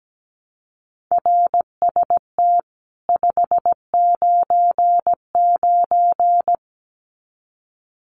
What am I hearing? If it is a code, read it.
Morse code, heard as a keyed tone, about 17 wpm, 700 Hz. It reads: RST 599
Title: Common QSO Phrases 17wpm